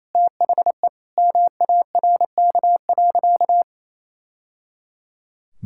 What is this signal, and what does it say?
Morse code, heard as a keyed tone, about 28 wpm, 700 Hz. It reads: THE MARK.
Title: Sentences from Top 1000 Words - Review Part 6 - 28wpm